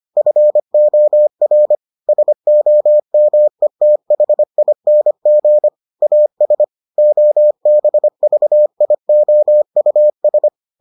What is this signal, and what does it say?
Morse code, heard as a keyed tone, about 25 wpm, 600 Hz. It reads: FOR SOMETHING AS OBVIOUS